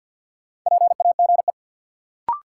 Morse code, heard as a keyed tone, about 53 wpm, 700 Hz. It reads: PAGE
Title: Sentences from Top 1000 Words - Review Part 1 - New Word Drill - 50wpm